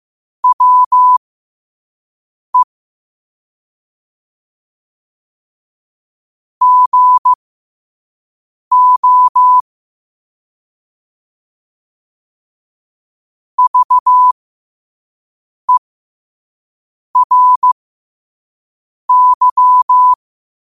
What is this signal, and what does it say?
Morse code, heard as a keyed tone, about 15 wpm, spaced as for 5 wpm, 1000 Hz. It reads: WE GO VERY
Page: New quotes every day in morse code at 5 Words per minute.